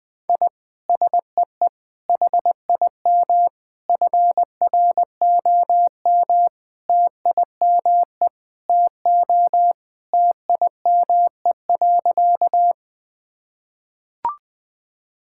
Morse code, heard as a keyed tone, about 21 wpm, 700 Hz. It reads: I SEE HIM FROM TIME TO TIME.
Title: Sentences with 4-Word Phrases - 20wpm